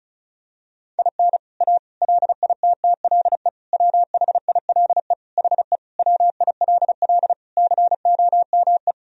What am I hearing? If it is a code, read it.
Morse code, heard as a keyed tone, about 36 wpm, 700 Hz. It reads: IN A LITTLE WHILE HE WILL COME
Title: Sentences from Top 1000 Words - Review Part 1 - 35wpm